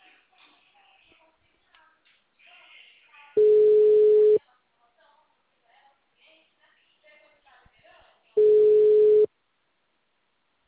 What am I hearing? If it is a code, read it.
Ringback tone (Europe, 425 Hz), with speech.